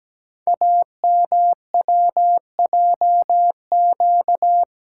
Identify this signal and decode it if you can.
Morse code, heard as a keyed tone, about 17 wpm, 700 Hz. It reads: AMWJQ
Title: Code Groups - 5 Letters - 17wpm